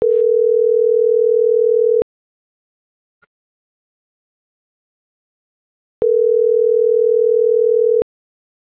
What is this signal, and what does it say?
Ringback tone (North America), 440 + 480 Hz.